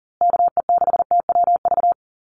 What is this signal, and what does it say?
Morse code, heard as a keyed tone, about 40 wpm, 700 Hz. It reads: KE6TWV
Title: USA Call Signs Repeated 2x 40wpm